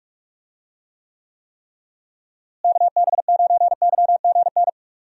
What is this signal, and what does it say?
Morse code, heard as a keyed tone, about 46 wpm, 700 Hz. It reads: KB9YGN
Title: USA Call Signs - 6 Characters - Repeated 3x - 45wpm